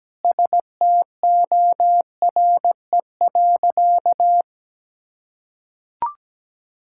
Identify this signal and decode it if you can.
Morse code, heard as a keyed tone, about 17 wpm, 700 Hz. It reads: STORE.
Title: Sentences from Top 1000 Words - Review Part 5 - 17wpm